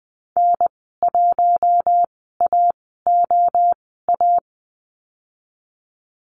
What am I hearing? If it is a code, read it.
Morse code, heard as a keyed tone, about 20 wpm, spaced as for 15 wpm, 700 Hz. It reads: N1AOA
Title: 062 Lesson 11g - Letter h - Call Signs 20wpm